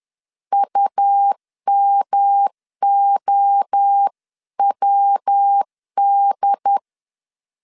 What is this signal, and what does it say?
Morse code, heard as a keyed tone, about 10 wpm, 800 Hz. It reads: UMOWD
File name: CW / DET / 1 3 1500C S B 10wpm 3335